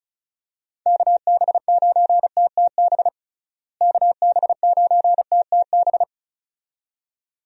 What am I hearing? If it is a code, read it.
Morse code, heard as a keyed tone, about 36 wpm, 700 Hz. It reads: KB9TTB KB9TTB
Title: USA Call Signs - 6 Characters - Repeated 2x - 35wpm